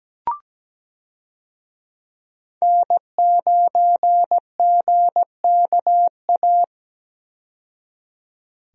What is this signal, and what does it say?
Morse code, heard as a keyed tone, about 17 wpm, 700 Hz. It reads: N9GKA
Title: USA Call Signs - 5 Characters - 17wpm